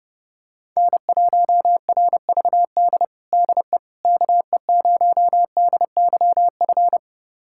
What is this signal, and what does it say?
Morse code, heard as a keyed tone, about 30 wpm, 700 Hz. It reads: N1RVD DE KE0DYF
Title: USA Callsign DE Callsign 30wpm